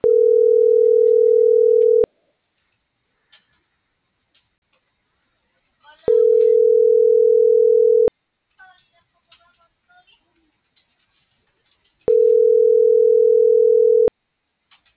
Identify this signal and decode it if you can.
Ringback tone (North America), 440 + 480 Hz.